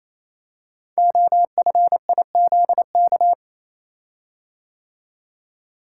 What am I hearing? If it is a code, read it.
Morse code, heard as a keyed tone, about 27 wpm, 700 Hz. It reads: OFIZK
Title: Code Groups - 5 Letters - 28wpm